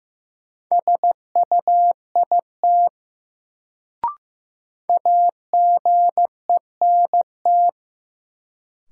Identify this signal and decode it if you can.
Morse code, heard as a keyed tone, about 15 wpm, 700 Hz. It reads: SUIT AGENT
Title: Sentences from Top 1600 Words - New Word Drill - 15wpm